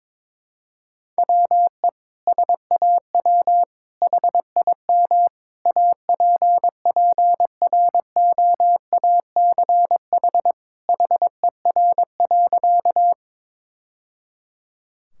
Morse code, heard as a keyed tone, about 22 wpm, 700 Hz. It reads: WE SAW HIM APPROACH HER.
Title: Sentences from Top 1000 Words - Review Part 6 - 22wpm